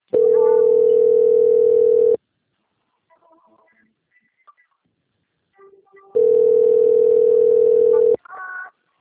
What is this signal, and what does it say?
Ringback tone (North America), 440 + 480 Hz.